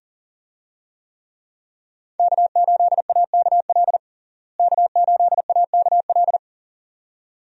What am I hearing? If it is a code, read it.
Morse code, heard as a keyed tone, about 40 wpm, 700 Hz. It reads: K8AKL K8AKL
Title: USA Call Signs - 5 Characters - Repeated 2x - 40wpm